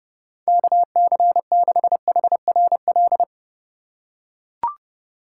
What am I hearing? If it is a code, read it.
Morse code, heard as a keyed tone, about 32 wpm, 700 Hz. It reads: KC6HRL
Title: USA Call Signs - 3x Character Spacing - 30wpm